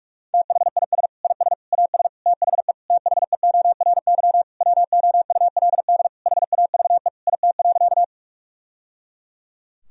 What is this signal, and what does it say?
Morse code, heard as a keyed tone, about 46 wpm, 700 Hz. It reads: THIS IS AS THE THEORY WOULD HAVE IT.
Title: Sentences from Top 1000 Words - Review Part 3 - 45wpm